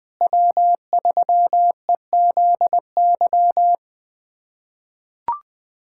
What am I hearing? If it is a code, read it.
Morse code, heard as a keyed tone, about 21 wpm, 700 Hz. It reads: W3EZY
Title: USA Callsign DE Callsign - 5 Characters - 20wpm